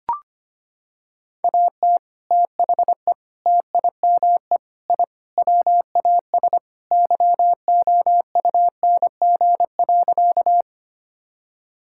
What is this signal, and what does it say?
Morse code, heard as a keyed tone, about 25 wpm, 700 Hz. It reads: AT THE TIME I WAS YOUNG.
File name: practice / 25wpm / Sentences with 3-Word Phrases - 25wpm